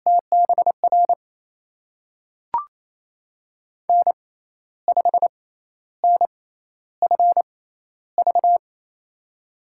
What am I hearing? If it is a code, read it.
Morse code, heard as a keyed tone, about 28 wpm, spaced as for 12 wpm, 700 Hz. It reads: TBR N5NFV
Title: USA Call Signs - 5 Characters - 6x Character Spacing - 28wpm